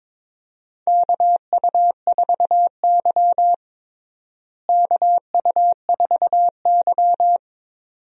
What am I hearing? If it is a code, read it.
Morse code, heard as a keyed tone, about 22 wpm, 700 Hz. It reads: KU4Y KU4Y
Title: USA Call Signs - 4 Characters - Repeated 2x - 22wpm